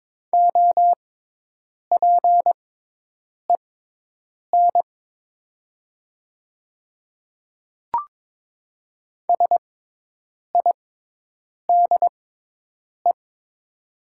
Morse code, heard as a keyed tone, about 22 wpm, spaced as for 8 wpm, 700 Hz. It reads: OPEN SIDE
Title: 6x Character Spacing - Top 300 Words - 22wpm